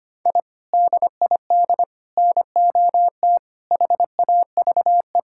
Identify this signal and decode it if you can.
Morse code, heard as a keyed tone, about 25 wpm, 700 Hz. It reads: I DID NOT HAVE
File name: practice / 25wpm / Sentences with 3-Word Phrases - 25wpm